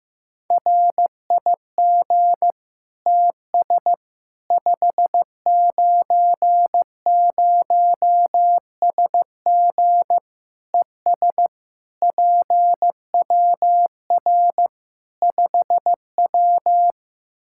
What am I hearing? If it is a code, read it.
Morse code, heard as a keyed tone, about 16 wpm, 700 Hz. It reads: RIG TS 590SG ES PWR 5W
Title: Practice QSO - Exchange 2 - Info - 15wpm